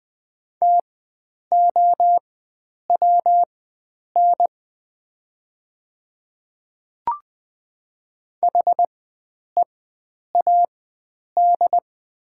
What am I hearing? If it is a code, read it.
Morse code, heard as a keyed tone, about 20 wpm, spaced as for 9 wpm, 700 Hz. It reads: TOWN HEAD
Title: 4x Character Spacing - Top 500 Words - 20wpm